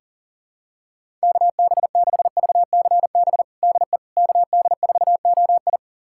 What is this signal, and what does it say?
Morse code, heard as a keyed tone, about 40 wpm, 700 Hz. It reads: KB6VCB DE KD4OI
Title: USA Callsign DE Callsign 40wpm